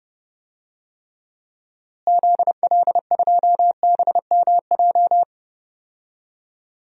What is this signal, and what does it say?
Morse code, heard as a keyed tone, about 30 wpm, 700 Hz. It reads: ZL2BMJ
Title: DXCC-100 Call Signs - 6 Characters - Repeated 3x - 30wpm